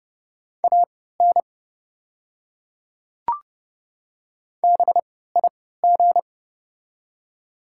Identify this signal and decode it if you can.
Morse code, heard as a keyed tone, about 29 wpm, spaced as for 17 wpm, 690 Hz. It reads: AN BIG
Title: 3x Character Spacing - Top 200 Words - 30wpm